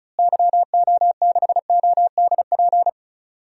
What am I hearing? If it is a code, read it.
Morse code, heard as a keyed tone, about 36 wpm, 700 Hz. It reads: YO6ODP
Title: DXCC-100 Call Signs - 6 Characters - Repeated 2x - 35wpm